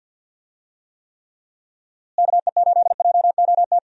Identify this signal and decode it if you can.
Morse code, heard as a keyed tone, about 52 wpm, 700 Hz. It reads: KE8JOT
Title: USA Call Signs - 6 Characters - Repeated 3x - 50wpm